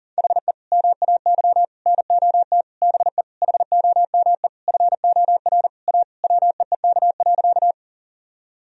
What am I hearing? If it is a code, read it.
Morse code, heard as a keyed tone, about 40 wpm, 700 Hz. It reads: HE MAY NOT BE HOME FOR A WEEK.
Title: Sentences from Top 1000 Words - Review Part 2 - 40wpm